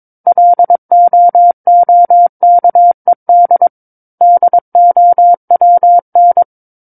Morse code, heard as a keyed tone, about 22 wpm, 700 Hz. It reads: LOOKED DOWN
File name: 06-Chapter-06 22 WPM